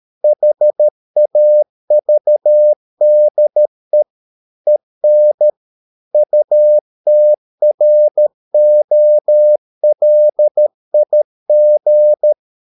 Morse code, heard as a keyed tone, about 13 wpm, 600 Hz. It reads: HAVDE EN UTROLIG
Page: Ferie 13 wpm